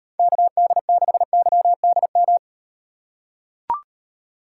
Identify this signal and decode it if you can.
Morse code, heard as a keyed tone, about 38 wpm, 700 Hz. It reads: KD6YDM